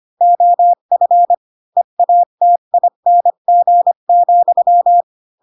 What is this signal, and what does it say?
Morse code, heard as a keyed tone, about 25 wpm, 700 Hz. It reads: OF EATING,
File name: Pinocchio / Pinocchio ch4 20 25 ik1hge